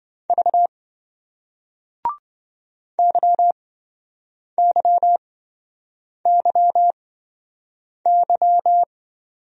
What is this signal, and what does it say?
Morse code, heard as a keyed tone, about 28 wpm, spaced as for 7 wpm, 700 Hz. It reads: V YYYY